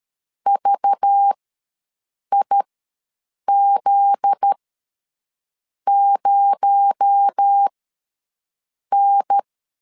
Morse code, heard as a keyed tone, about 13 wpm, spaced as for 6 wpm, 800 Hz. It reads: VIZ0N
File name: LEZIONI / 7 GF9Z / 7 GF9Z ws 12wpm 3321